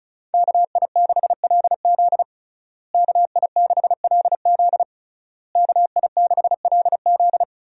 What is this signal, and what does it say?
Morse code, heard as a keyed tone, about 36 wpm, 700 Hz. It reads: KI6LZ KI6LZ KI6LZ
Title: USA Call Signs - 5 Characters - Repeated 3x - 35wpm